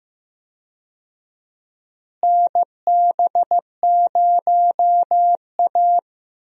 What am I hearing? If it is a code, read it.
Morse code, heard as a keyed tone, about 15 wpm, 700 Hz. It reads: NB0A
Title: USA Call Signs - 4 Characters - Repeated 3x - 15wpm